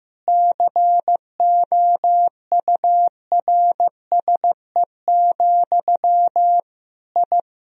Morse code, heard as a keyed tone, about 16 wpm, 700 Hz. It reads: COURSE, I
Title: Sentences from Top 1000 Words - Review Part 3 - 15wpm